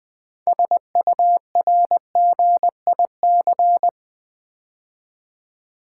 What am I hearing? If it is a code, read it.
Morse code, heard as a keyed tone, about 20 wpm, 700 Hz. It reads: SURGIC
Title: Build A Word - 2 Letters At A Time - 8-Letter Words - 20wpm